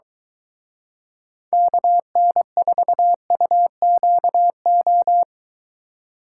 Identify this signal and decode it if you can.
Morse code, heard as a keyed tone, about 22 wpm, 700 Hz. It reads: KN4UQO